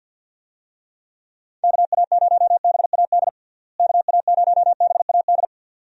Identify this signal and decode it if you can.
Morse code, heard as a keyed tone, about 47 wpm, 700 Hz. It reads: KA0BAD KA0BAD
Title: USA Call Signs - 6 Characters - Repeated 2x - 50wpm